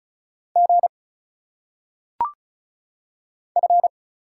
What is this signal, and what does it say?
Morse code, heard as a keyed tone, about 36 wpm, 700 Hz. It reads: G F
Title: Single Letters - Phonetic Pronunciation 35wpm